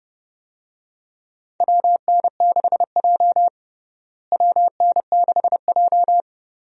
Morse code, heard as a keyed tone, about 30 wpm, 700 Hz. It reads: WN6J WN6J
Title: USA Call Signs - 4 Characters - Repeated 2x - 30wpm